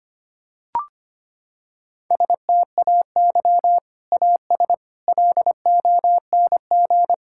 Morse code, heard as a keyed tone, about 25 wpm, 700 Hz. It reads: STAY AS LONG
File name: practice / 25wpm / Sentences with 3-Word Phrases - 25wpm